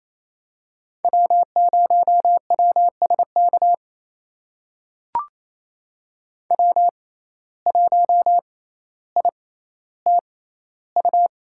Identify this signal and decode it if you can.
Morse code, heard as a keyed tone, about 27 wpm, spaced as for 14 wpm, 700 Hz. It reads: W0WSK W1ITU
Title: USA Call Signs - 5 Characters - 6x Character Spacing - 28wpm